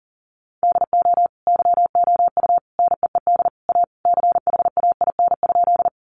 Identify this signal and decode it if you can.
Morse code, heard as a keyed tone, about 40 wpm, 700 Hz. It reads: DO YOU NEED A CHAIN?
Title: Sentences from Top 1500 Words 40wpm